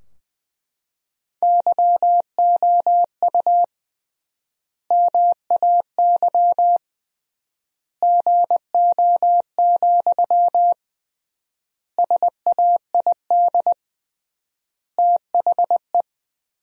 Morse code, heard as a keyed tone, about 21 wpm, 700 Hz. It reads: YOU MAY GO, SAID THE
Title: 3x Word-Spacing - Sentences from Top 400 Words - 20wpm